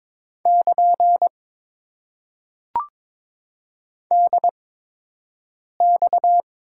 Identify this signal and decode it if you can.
Morse code, heard as a keyed tone, about 22 wpm, spaced as for 6 wpm, 700 Hz. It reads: <KN> DX
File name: Top 90 QSO Elements - 8x Character Spacing - 22wpm